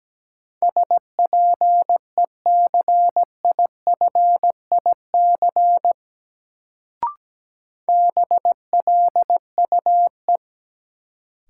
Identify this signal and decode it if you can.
Morse code, heard as a keyed tone, about 17 wpm, 700 Hz. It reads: SPECIFIC BLUE
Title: Sentences from Top 1000 Words - Review Part 3 - New Word Drill - 17wpm